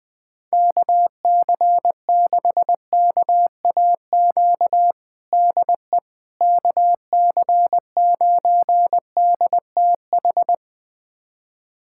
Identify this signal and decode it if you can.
Morse code, heard as a keyed tone, about 21 wpm, 700 Hz. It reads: KC6KAQ DE KC9DTH
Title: USA Callsign DE Callsign - 6 Characters - 20wpm